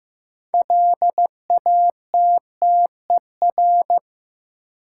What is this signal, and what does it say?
Morse code, heard as a keyed tone, about 15 wpm, 700 Hz. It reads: LATTER
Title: Sentences from Top 1600 Words - New Word Drill - 15wpm